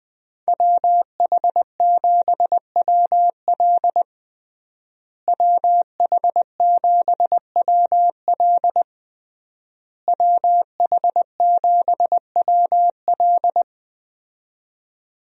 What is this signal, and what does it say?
Morse code, heard as a keyed tone, about 20 wpm, 700 Hz. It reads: WH7WL WH7WL WH7WL
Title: USA Call Signs - 5 Characters - Repeated 3x - 20wpm